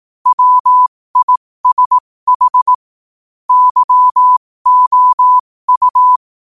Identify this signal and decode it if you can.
Morse code, heard as a keyed tone, about 18 wpm, spaced as for 15 wpm, 1000 Hz. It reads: WISH YOU
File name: qotdfiles / 15wpm